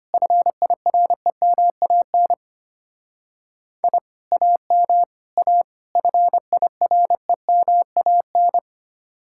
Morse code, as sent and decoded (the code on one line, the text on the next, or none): ..-. .. .-. -- .- -. / .. / .- -- / .- / ..-. .. .-. -- .- -.
FIREMAN I AM A FIREMAN